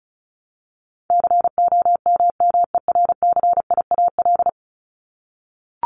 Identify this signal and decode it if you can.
Morse code, heard as a keyed tone, about 34 wpm, 700 Hz. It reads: COMMERCIAL
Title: Top 501-750 Words 35wpm